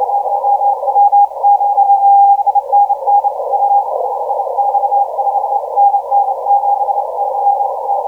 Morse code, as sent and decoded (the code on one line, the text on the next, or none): none